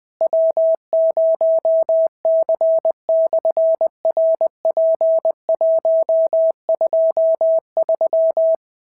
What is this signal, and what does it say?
Morse code, heard as a keyed tone, about 20 wpm, 650 Hz. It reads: W0C/RP123
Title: SOTA QSO Practice 20wpm